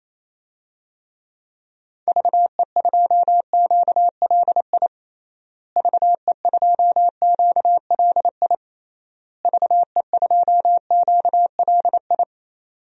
Morse code, heard as a keyed tone, about 28 wpm, 700 Hz. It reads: VE2QLI VE2QLI VE2QLI
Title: DXCC-100 Call Signs - 6 Characters - Repeated 3x - 28wpm